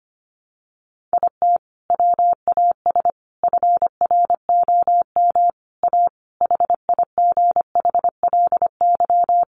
Morse code, heard as a keyed tone, about 25 wpm, 690 Hz. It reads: IT WAS FROM A HIGHLY
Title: Sentences from Top 1100 Words 25wpm